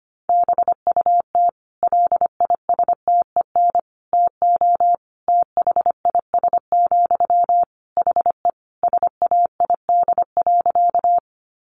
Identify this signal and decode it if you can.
Morse code, heard as a keyed tone, about 25 wpm, 700 Hz. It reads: BUT LISTEN TO THIS, HE SAID.
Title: Sentences from Top 400 Words 25wpm